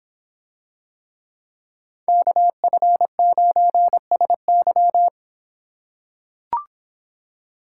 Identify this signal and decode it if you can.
Morse code, heard as a keyed tone, about 26 wpm, 700 Hz. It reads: KF9SY